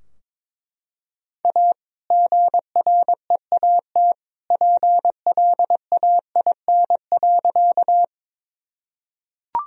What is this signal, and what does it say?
Morse code, heard as a keyed tone, about 21 wpm, 700 Hz. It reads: A GREAT PLAIN.
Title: Sentences from Top 1000 Words - Review Part 2 - 22wpm